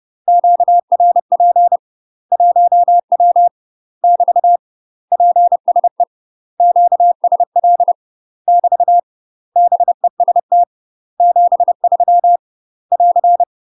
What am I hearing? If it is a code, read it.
Morse code, heard as a keyed tone, about 30 wpm, 700 Hz. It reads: QRP 1W <BT> PSE QSL <BT> BEST 73 <AR>
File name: UK MASTER QSOs TRAINING / 30WPM / UK MASTER QSOs 0003